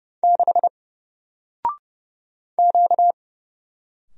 Morse code, heard as a keyed tone, about 31 wpm, 700 Hz. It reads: 6 Q
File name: Single-Letter-Number-30wpm